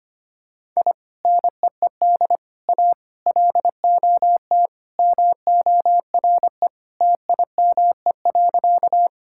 Morse code, heard as a keyed tone, about 25 wpm, 700 Hz. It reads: I NEED A LOT MORE TIME.
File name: Sentences with 3-Word Phrases - 25wpm